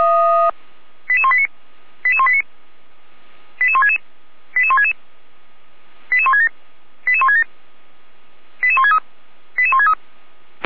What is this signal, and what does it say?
Five-tone selective call (ZVEI), heard as quick runs of five tones, tones of about 70 ms. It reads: 88189 88189 88166 88166 88167 88167 88163 88163
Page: /455/SDR/ZVEI/